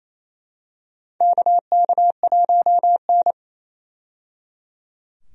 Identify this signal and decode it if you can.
Morse code, heard as a keyed tone, about 28 wpm, 700 Hz. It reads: KK1N
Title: USA Call Signs - 4 Characters - 28wpm